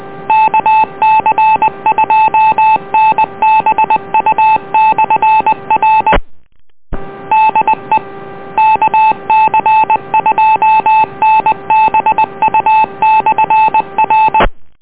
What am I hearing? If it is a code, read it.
Morse code, heard as a keyed tone, about 20 wpm, 870 Hz. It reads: KC2NBU/R DE KC2NBU/R